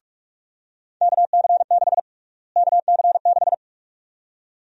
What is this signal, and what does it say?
Morse code, heard as a keyed tone, about 45 wpm, 700 Hz. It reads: KC6 KC6
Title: USA Call Sign Prefixes - Repeated 2x - 45wpm